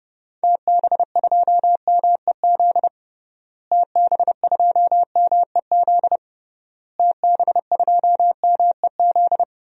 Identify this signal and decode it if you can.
Morse code, heard as a keyed tone, about 30 wpm, 700 Hz. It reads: TB2MEZ TB2MEZ TB2MEZ
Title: DXCC-100 Call Signs - 6 Characters - Repeated 3x - 30wpm